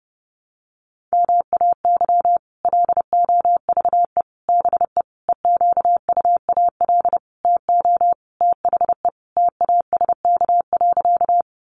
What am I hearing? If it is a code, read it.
Morse code, heard as a keyed tone, about 30 wpm, 700 Hz. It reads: MAY LOVE BE EQUAL TO THE TASK.
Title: Sentences from Top 1400 Words 30wpm